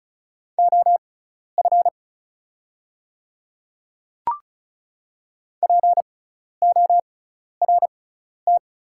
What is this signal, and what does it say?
Morse code, heard as a keyed tone, about 35 wpm, spaced as for 12 wpm, 690 Hz. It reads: OF PORT
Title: 6x Character Spacing - Top 200 Words - 35wpm